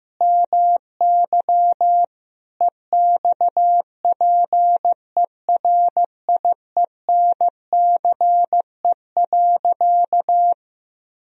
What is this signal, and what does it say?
Morse code, heard as a keyed tone, about 15 wpm, 700 Hz. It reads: MY EXPERIENCE.
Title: Sentences from Top 1000 Words - Review Part 3 - 15wpm